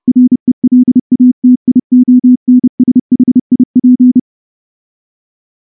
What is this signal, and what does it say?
Morse code, heard as a keyed tone, about 30 wpm, 250 Hz. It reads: RELATIONSHIP
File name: morse / morse-training-30WPM